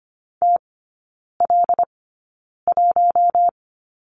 Morse code, heard as a keyed tone, about 25 wpm, spaced as for 9 wpm, 690 Hz. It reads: TL1
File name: Lesson 15e - Number 5 - Three Characters 25wpm